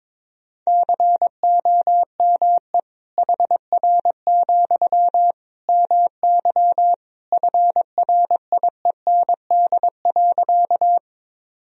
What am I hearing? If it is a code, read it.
Morse code, heard as a keyed tone, about 21 wpm, 700 Hz. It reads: COME HR, MY FRIEND.
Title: Sentences from Top 300 Words and Abbreviations 22wpm